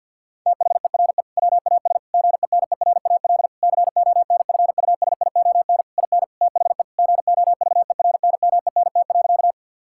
Morse code, heard as a keyed tone, about 50 wpm, 700 Hz. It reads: THERE WAS GENERAL CONFUSION IN THE GOVERNMENT.
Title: Sentences from Top 1600 Words 50wpm